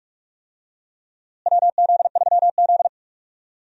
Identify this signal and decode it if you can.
Morse code, heard as a keyed tone, about 43 wpm, 700 Hz. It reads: WZ3Z
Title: USA Call Signs - 4 Characters - Repeated 2x - 45wpm